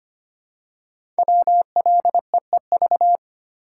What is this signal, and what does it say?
Morse code, heard as a keyed tone, about 25 wpm, 700 Hz. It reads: WLEEV
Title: Code Groups - 5 Letters - 25wpm